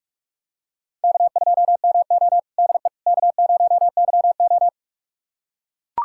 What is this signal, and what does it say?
Morse code, heard as a keyed tone, about 42 wpm, 700 Hz. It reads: K2MO DE K0YO
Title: USA Callsign DE Callsign - 4 Characters - 45wpm